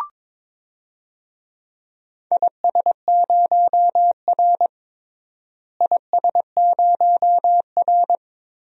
Morse code, heard as a keyed tone, about 21 wpm, 700 Hz. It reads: IS0R IS0R
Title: DXCC-100 Call Signs - 4 Characters - Repeated 2x - 22wpm